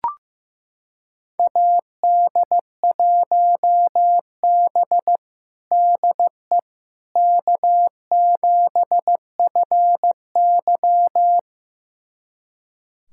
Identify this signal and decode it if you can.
Morse code, heard as a keyed tone, about 15 wpm, 700 Hz. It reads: AD1B DE K7FY
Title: USA Callsign DE Callsign - 4 Characters - 15wpm